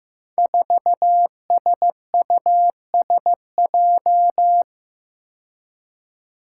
Morse code, heard as a keyed tone, about 15 wpm, 700 Hz. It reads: 4SUSJ
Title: Five Challenging Letter-Number Combinations 15wpm